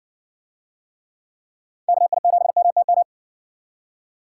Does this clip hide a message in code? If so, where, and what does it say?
Yes: Morse code from 1.9 to 3.0 s, heard as a keyed tone, about 58 wpm, 700 Hz. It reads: KI7MTM